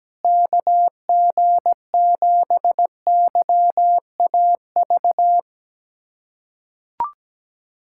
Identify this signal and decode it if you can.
Morse code, heard as a keyed tone, about 17 wpm, 700 Hz. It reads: KG7YAV
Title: USA Callsign DE Callsign 17wpm